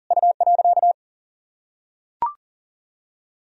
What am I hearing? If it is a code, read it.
Morse code, heard as a keyed tone, about 37 wpm, 700 Hz. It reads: U.
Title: Sentences from Top 300 Words and Abbreviations 40wpm